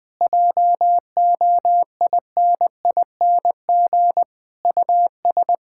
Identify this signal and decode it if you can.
Morse code, heard as a keyed tone, about 20 wpm, 700 Hz. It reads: JOINING US
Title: Build A Phrase - 5-Word Phrases - 20wpm